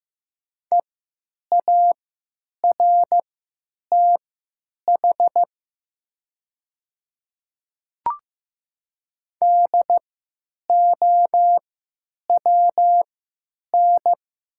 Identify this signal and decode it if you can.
Morse code, heard as a keyed tone, about 15 wpm, spaced as for 9 wpm, 700 Hz. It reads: EARTH DOWN
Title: 3x Character Spacing - Top 200 Words - 15wpm